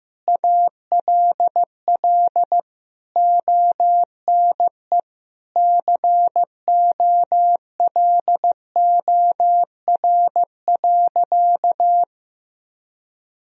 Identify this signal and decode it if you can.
Morse code, heard as a keyed tone, about 15 wpm, 700 Hz. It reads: ALL ONE COLOR.
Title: Sentences from Top 1000 Words - Review Part 2 - 15wpm